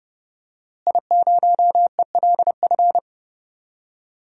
Morse code, heard as a keyed tone, about 30 wpm, 700 Hz. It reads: I0ELF
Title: DXCC-100 Call Signs - 5 Characters - Repeated 2x - 30wpm